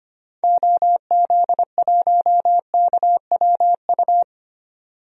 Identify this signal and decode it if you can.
Morse code, heard as a keyed tone, about 25 wpm, 700 Hz. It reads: OZ1KWU
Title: DXCC-100 Call Signs - 6 Characters - 25wpm